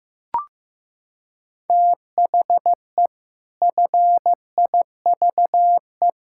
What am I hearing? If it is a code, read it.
Morse code, heard as a keyed tone, about 15 wpm, 700 Hz. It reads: THE FIVE
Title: Pangram Challenge v2 - 15wpm